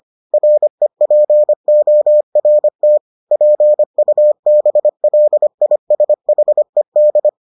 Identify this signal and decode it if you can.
Morse code, heard as a keyed tone, about 25 wpm, 600 Hz. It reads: REPORT PUBLISHED